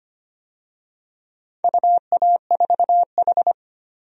Morse code, heard as a keyed tone, about 25 wpm, 700 Hz. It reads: UA4H
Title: DXCC-100 Call Signs - 4 Characters - 25wpm